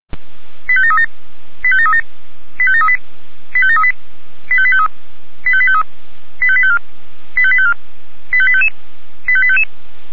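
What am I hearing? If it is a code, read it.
Five-tone selective call (ZVEI), heard as quick runs of five tones, tones of about 80 ms. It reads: 85737 85737 85738 85738 85853 85853 85854 85854 85855 85855
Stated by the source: ZVEI five-tone selective call received on SDR.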